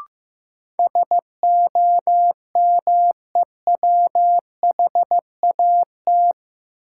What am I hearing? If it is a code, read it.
Morse code, heard as a keyed tone, about 15 wpm, 700 Hz. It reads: SOMEWHAT
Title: Sentences from Top 1000 Words - Review Part 1 - New Word Drill - 15wpm